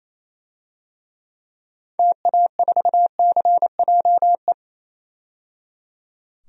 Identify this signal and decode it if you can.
Morse code, heard as a keyed tone, about 27 wpm, 700 Hz. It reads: TA4CJE
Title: DXCC-100 Call Signs - 6 Characters - 28wpm